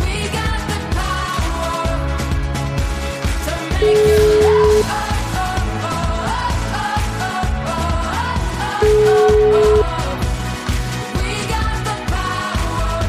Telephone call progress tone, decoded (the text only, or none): ringback tone